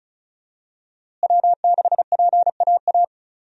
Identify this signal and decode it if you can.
Morse code, heard as a keyed tone, about 36 wpm, 700 Hz. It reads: W6PAA
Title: USA Call Signs - 5 Characters - Repeated 2x - 35wpm